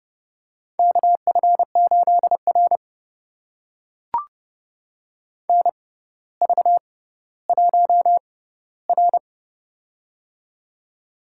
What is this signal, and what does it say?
Morse code, heard as a keyed tone, about 29 wpm, spaced as for 15 wpm, 700 Hz. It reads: KF8R NV1R
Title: USA Call Signs - 4 Characters - 6x Character Spacing - 30wpm